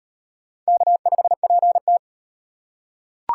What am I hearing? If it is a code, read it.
Morse code, heard as a keyed tone, about 38 wpm, 700 Hz. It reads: K5PT